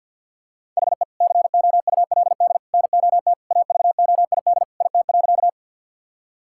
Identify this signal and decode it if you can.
Morse code, heard as a keyed tone, about 50 wpm, 700 Hz. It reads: HE COULD NOT AVOID IT.
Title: Sentences from Top 1000 Words - Review Part 6 - 50wpm